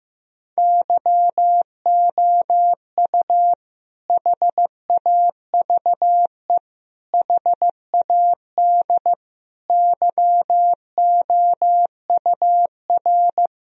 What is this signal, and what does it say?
Morse code, heard as a keyed tone, about 15 wpm, 700 Hz. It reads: YOU HAVE HAD YOUR
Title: Sentences from Top 1000 Words - Review Part 1 - 15wpm